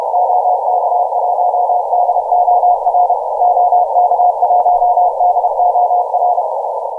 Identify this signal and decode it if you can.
Morse code, heard as a keyed tone, about 31 wpm, 720 Hz, with noise. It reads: OK1DOR OK1DOR